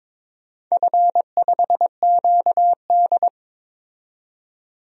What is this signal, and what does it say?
Morse code, heard as a keyed tone, about 22 wpm, 700 Hz. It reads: F5QD
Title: DXCC-100 Call Signs - 4 Characters - 22wpm